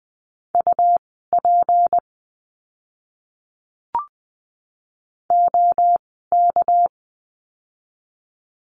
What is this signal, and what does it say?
Morse code, heard as a keyed tone, about 19 wpm, spaced as for 14 wpm, 700 Hz. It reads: UP OK
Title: 199 Lesson 32f - Letter k - Words 20wpm